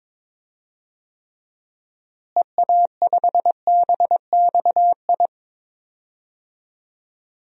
Morse code, heard as a keyed tone, about 22 wpm, 700 Hz. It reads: EA5BXI
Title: DXCC-100 Call Signs - 6 Characters - Repeated 3x - 22wpm